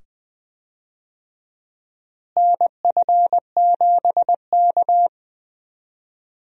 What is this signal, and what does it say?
Morse code, heard as a keyed tone, about 20 wpm, 700 Hz. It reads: NF7K
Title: USA Call Signs - 4 Characters - Repeated 2x - 20wpm